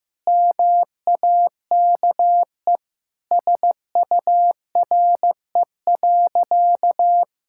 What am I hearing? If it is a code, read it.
Morse code, heard as a keyed tone, about 15 wpm, 700 Hz. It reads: MAKE SURE.
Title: Sentences from Top 1000 Words - Review Part 5 - 15wpm